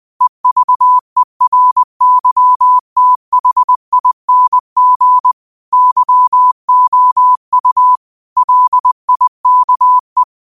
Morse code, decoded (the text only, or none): EVERYTHING YOU LIKE